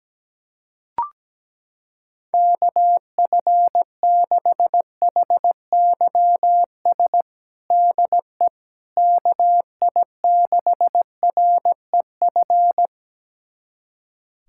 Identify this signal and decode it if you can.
Morse code, heard as a keyed tone, about 17 wpm, 700 Hz. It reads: KF6HYS DE KI6REF
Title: USA Callsign DE Callsign - 6 Characters - 17wpm